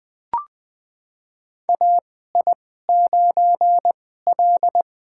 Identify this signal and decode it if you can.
Morse code, heard as a keyed tone, about 20 wpm, spaced as for 14 wpm, 700 Hz. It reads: AI9L
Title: USA Call Signs - 2x Character Spacing - 20wpm